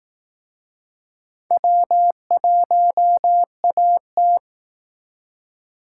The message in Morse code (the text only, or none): W1AT